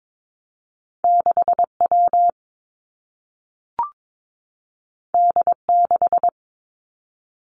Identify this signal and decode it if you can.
Morse code, heard as a keyed tone, about 22 wpm, 700 Hz. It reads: 6 W D 6
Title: Two Challenging Letter-Number Combinations 22wpm